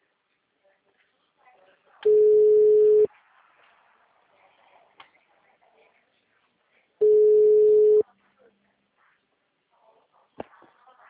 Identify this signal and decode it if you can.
Ringback tone (Europe, 425 Hz), with speech.